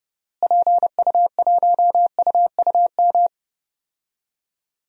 Morse code, heard as a keyed tone, about 30 wpm, 700 Hz. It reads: PU1UUM
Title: DXCC-100 Call Signs - 6 Characters - Repeated 2x - 30wpm